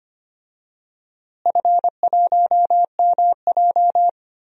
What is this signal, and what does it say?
Morse code, heard as a keyed tone, about 25 wpm, 700 Hz. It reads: F1MJ
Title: DXCC-100 Call Signs - 4 Characters - Repeated 2x - 25wpm